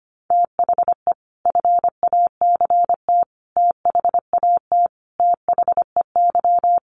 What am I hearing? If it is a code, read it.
Morse code, heard as a keyed tone, about 25 wpm, 700 Hz. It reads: THE FACT THAT THEY
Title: Sets of 5 Words Encore2 25wpm